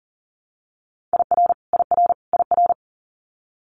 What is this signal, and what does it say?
Morse code, heard as a keyed tone, about 40 wpm, 700 Hz. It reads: IR IR IR
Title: Multi-letter Phonemes Repeated 3x 40wpm